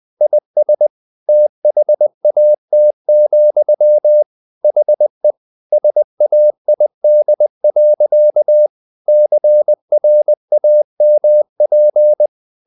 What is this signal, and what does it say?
Morse code, heard as a keyed tone, about 20 wpm, 600 Hz. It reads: IS THAT, HE SAID. CRAMP
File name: the old man and the sea / the old man and the sea part 1 20wpm